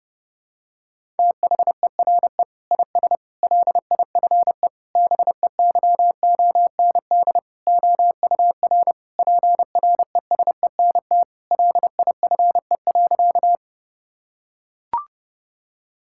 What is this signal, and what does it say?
Morse code, heard as a keyed tone, about 32 wpm, 700 Hz. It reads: THERE IS LIFE BEYOND OUR PRESENT LIFE.
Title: Sentences from Top 1000 Words - Review Part 6 - 30wpm